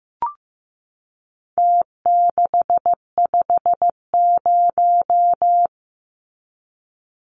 Morse code, heard as a keyed tone, about 15 wpm, 700 Hz. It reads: T650
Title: Numbers - Contest Style - 4-Digit 15wpm